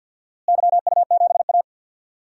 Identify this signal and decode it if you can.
Morse code, heard as a keyed tone, about 46 wpm, 700 Hz. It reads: YU7A
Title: DXCC-100 Call Signs - 4 Characters - Repeated 3x - 50wpm